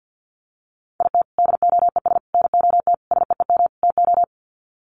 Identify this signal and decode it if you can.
Morse code, heard as a keyed tone, about 50 wpm, 700 Hz. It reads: IT DOES NOT SEEM TO
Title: Sets of 5 Words Encore2 50wpm